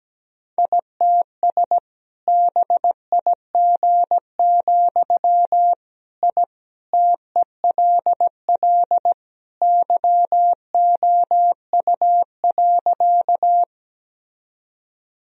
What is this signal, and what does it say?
Morse code, heard as a keyed tone, about 17 wpm, 700 Hz. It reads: ITS BIG, I TELL YOU.
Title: Sentences from Top 1000 Words - Review Part 1 - 17wpm